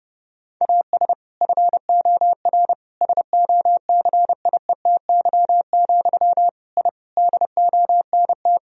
Morse code, heard as a keyed tone, about 30 wpm, 700 Hz. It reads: AS FOR SOCIETY, I DONT
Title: Sentences from Top 1000 Words - Review Part 4 - 30wpm